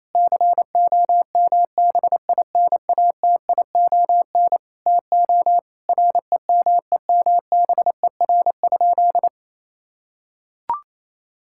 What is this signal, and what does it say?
Morse code, heard as a keyed tone, about 28 wpm, 700 Hz. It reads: COMBINATION TO REMEMBER?
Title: Sentences from Top 1000 Words - Review Part 6 - 28wpm